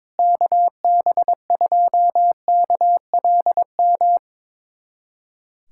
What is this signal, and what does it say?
Morse code, heard as a keyed tone, about 22 wpm, 700 Hz. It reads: KB2KLM
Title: USA Callsign DE Callsign - 6 Characters - 22wpm